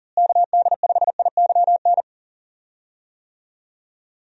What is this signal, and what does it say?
Morse code, heard as a keyed tone, about 40 wpm, 700 Hz. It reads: KD5IYN
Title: USA Call Signs - 6 Characters - Repeated 3x - 40wpm